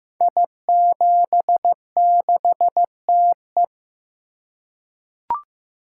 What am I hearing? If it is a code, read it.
Morse code, heard as a keyed tone, about 15 wpm, 700 Hz. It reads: I76TE
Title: Code Groups - 5 Letter:Numbers - 15wpm